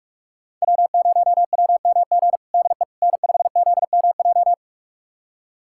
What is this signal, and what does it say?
Morse code, heard as a keyed tone, about 43 wpm, 700 Hz. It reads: W0WMG DE N5ZMJ
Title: USA Callsign DE Callsign - 5 Characters - 45wpm